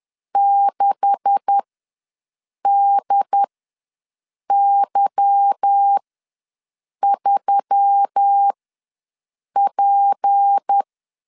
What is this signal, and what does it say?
Morse code, heard as a keyed tone, about 10 wpm, spaced as for 6 wpm, 800 Hz. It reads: 6DY3P
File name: LEZIONI / 9 286 ws 10wpm 4032